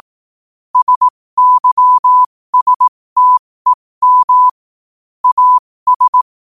Morse code, heard as a keyed tone, about 18 wpm, spaced as for 15 wpm, 1000 Hz. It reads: SYSTEM AS
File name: qotdfiles / 15wpm